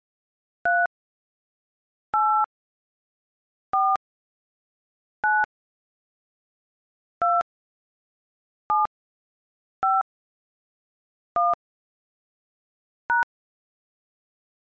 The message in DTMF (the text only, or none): 3 8 4 9 2 7 5 1 #